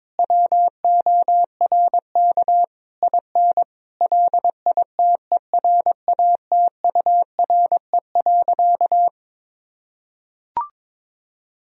Morse code, heard as a keyed tone, about 22 wpm, 700 Hz. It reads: WORK IN LITERATURE.
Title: Sentences from Top 1000 Words - Review Part 3 - 22wpm